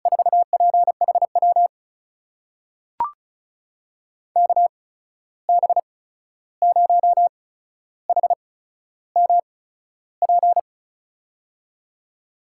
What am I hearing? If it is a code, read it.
Morse code, heard as a keyed tone, about 36 wpm, spaced as for 13 wpm, 700 Hz. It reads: 4PHW KB0HMP
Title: USA Call Signs - 6 Characters - 8x Character Spacing - 35wpm